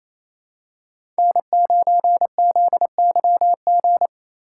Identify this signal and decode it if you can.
Morse code, heard as a keyed tone, about 27 wpm, 700 Hz. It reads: N9ZYG
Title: USA Call Signs - 2x Character Spacing - 28wpm